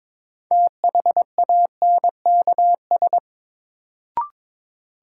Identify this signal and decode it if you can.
Morse code, heard as a keyed tone, about 21 wpm, 700 Hz. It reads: THANKS
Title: Sets of 5 Words - New Word Drill - 22wpm